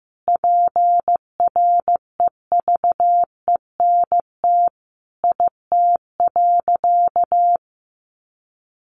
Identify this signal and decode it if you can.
Morse code, heard as a keyed tone, about 15 wpm, 700 Hz. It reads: PREVENT IT.
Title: Sentences from Top 1100 Words 15wpm